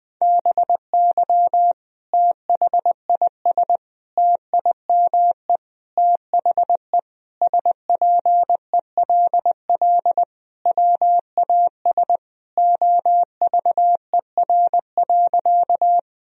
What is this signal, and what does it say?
Morse code, heard as a keyed tone, about 20 wpm, 700 Hz. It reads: BY THIS TIME THE SPELL WAS OVER.
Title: Sentences from Top 1000 Words - Review Part 1 - 20wpm